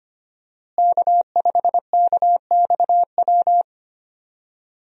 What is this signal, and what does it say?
Morse code, heard as a keyed tone, about 25 wpm, 700 Hz. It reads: K5KXW
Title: USA Call Signs - 5 Characters - Repeated 3x - 25wpm